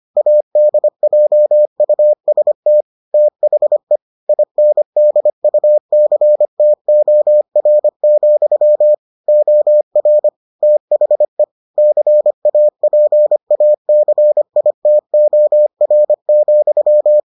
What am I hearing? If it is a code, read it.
Morse code, heard as a keyed tone, about 25 wpm, 600 Hz. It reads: ADJUST THE INDUCTOR, OR THE CAPACITOR,